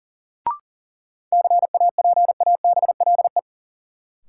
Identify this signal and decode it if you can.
Morse code, heard as a keyed tone, about 40 wpm, 700 Hz. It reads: CAPABLE
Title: Sentences from Top 1200 Words - New Word Drill - 40wpm